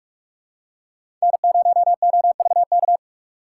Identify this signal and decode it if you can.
Morse code, heard as a keyed tone, about 46 wpm, 690 Hz. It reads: N0OVK
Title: USA Call Signs - 5 Characters - 8x Character Spacing - 45wpm